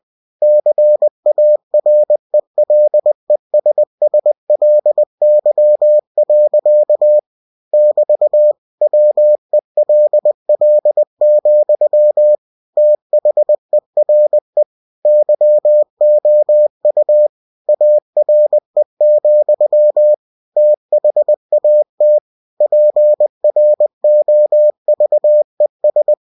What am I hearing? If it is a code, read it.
Morse code, heard as a keyed tone, about 20 wpm, 600 Hz. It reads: CARELESSLY. <BT> WELL, THERE YOU ARE, THAT PROVES
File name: cw-books / winnie-the-pooh / winnie-20wpm-page-0021